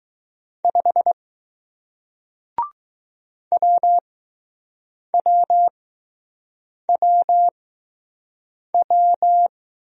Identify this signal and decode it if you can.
Morse code, heard as a keyed tone, about 20 wpm, spaced as for 6 wpm, 700 Hz. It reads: 5 WWWW